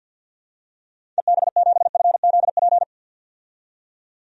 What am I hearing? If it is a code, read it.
Morse code, heard as a keyed tone, about 51 wpm, 700 Hz. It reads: EB7FZP
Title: DXCC-100 Call Signs - 6 Characters - 50wpm